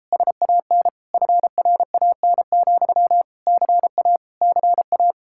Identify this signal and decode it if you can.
Morse code, heard as a keyed tone, about 33 wpm, 700 Hz. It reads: SAN FRAN, CA CA